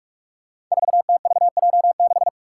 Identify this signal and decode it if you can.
Morse code, heard as a keyed tone, about 46 wpm, 700 Hz. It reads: 4TVJ6
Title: Five Challenging Letter-Number Combinations 45wpm